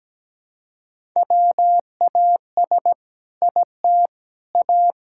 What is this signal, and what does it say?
Morse code, heard as a keyed tone, about 17 wpm, 700 Hz. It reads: WAS IT A
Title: Sentences from Top 1000 Words - Review Part 6 - 17wpm